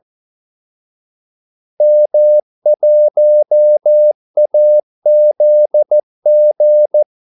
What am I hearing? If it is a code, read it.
Morse code, heard as a keyed tone, about 14 wpm, 600 Hz. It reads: M1AZG
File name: Callsigns-14wpm